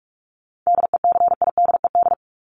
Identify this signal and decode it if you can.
Morse code, heard as a keyed tone, about 44 wpm, 700 Hz. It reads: DECIDED
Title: ED Suffix - Top 100 Words with Base Words 45wpm